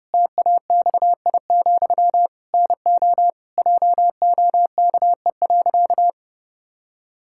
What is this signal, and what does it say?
Morse code, heard as a keyed tone, about 30 wpm, 700 Hz. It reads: TAXI, NO JOKE.
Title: Pangram Challenge v2 - 30wpm